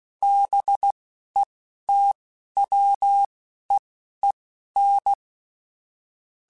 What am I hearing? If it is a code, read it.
Morse code, heard as a keyed tone, about 16 wpm, spaced as for 12 wpm, 800 Hz. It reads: BETWEEN
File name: Weather Sets / Weather Sets-mp3s / Weather 5 1 @10